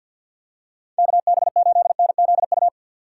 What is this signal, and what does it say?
Morse code, heard as a keyed tone, about 53 wpm, 700 Hz. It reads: KB8NZU
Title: USA Call Signs - 6 Characters - 6x Character Spacing - 50wpm